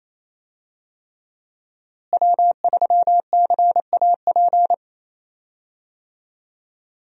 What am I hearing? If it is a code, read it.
Morse code, heard as a keyed tone, about 28 wpm, 700 Hz. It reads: W3CAP
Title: USA Call Signs - 5 Characters - Repeated 3x - 28wpm